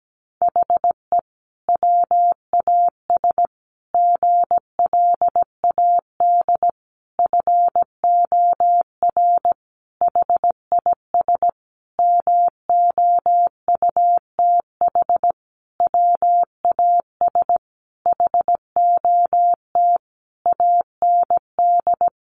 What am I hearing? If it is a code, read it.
Morse code, heard as a keyed tone, about 17 wpm, 700 Hz. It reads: HE WAS GLAD FOR HIS MOUTH WAS HOT AND
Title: Sentences from Top 1500 Words 17wpm